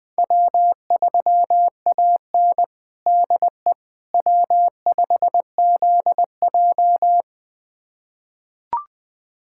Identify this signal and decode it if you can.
Morse code, heard as a keyed tone, about 20 wpm, 700 Hz. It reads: W3AN DE W5ZJ
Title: USA Callsign DE Callsign - 4 Characters - 20wpm